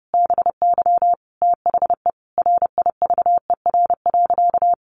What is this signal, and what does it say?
Morse code, heard as a keyed tone, about 28 wpm, 700 Hz. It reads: BY THE RIVER.
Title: Sentences from Top 1200 Words 30wpm